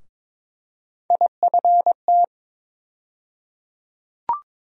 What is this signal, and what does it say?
Morse code, heard as a keyed tone, about 22 wpm, 700 Hz. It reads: IFT
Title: USA Call Sign Postfixes - 3 Letters 22wpm